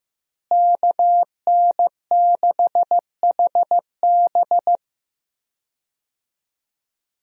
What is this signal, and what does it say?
Morse code, heard as a keyed tone, about 15 wpm, 700 Hz. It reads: KN6HB
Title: USA Call Signs - 5 Characters - Repeated 3x - 15wpm